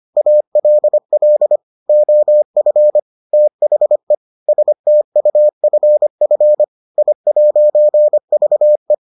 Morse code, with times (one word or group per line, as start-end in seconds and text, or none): 0.2-1.6 ALL
1.9-3.0 OF
3.3-4.1 THE
4.5-6.6 STUFF
7.0-8.9 I'VE